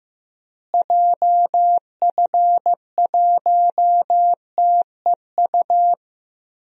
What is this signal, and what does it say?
Morse code, heard as a keyed tone, about 14 wpm, 700 Hz. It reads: JF1TEU
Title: DXCC-100 Call Signs - 6 Characters - Repeated 2x - 15wpm